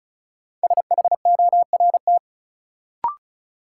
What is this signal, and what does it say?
Morse code, heard as a keyed tone, about 36 wpm, 700 Hz. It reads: SHORT
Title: Sets of 5 Words Encore - New Word Drill - 35wpm